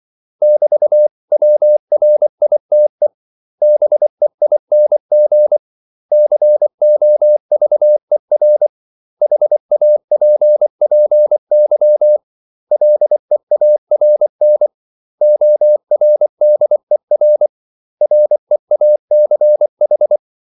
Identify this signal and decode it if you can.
Morse code, heard as a keyed tone, about 24 wpm, 600 Hz. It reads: <BT>WRITE BEING COVER HAPPY LEARN ORDER REACH
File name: CW / Practice / 500MostCommonEnglishWords24